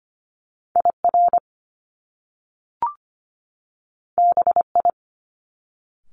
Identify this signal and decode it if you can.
Morse code, heard as a keyed tone, about 25 wpm, 700 Hz. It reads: IR BI
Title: Top Two Letter Combinations in Words 25wpm